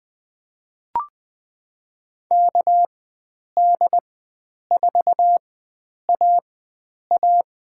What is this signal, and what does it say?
Morse code, heard as a keyed tone, about 21 wpm, spaced as for 9 wpm, 700 Hz. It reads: KD4AA
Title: USA Call Signs - 4x Character Spacing - 20wpm